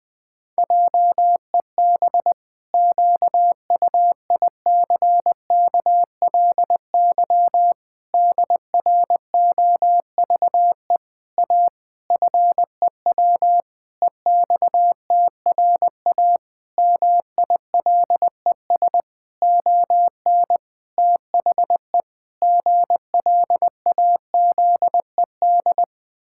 Morse code, heard as a keyed tone, about 20 wpm, 700 Hz. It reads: JEB QUICKLY DROVE A FEW EXTRA MILES ON THE GLAZED
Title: Pangram Challenge v2 - 20wpm